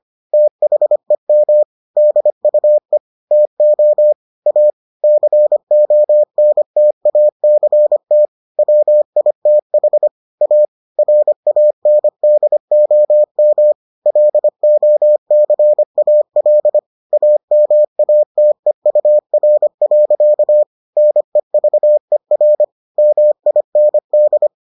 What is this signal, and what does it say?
Morse code, heard as a keyed tone, about 25 wpm, 600 Hz. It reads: THEM DUE TO A CONTACT WITH A RANDOM LOCAL AMATEUR. NEVER MIND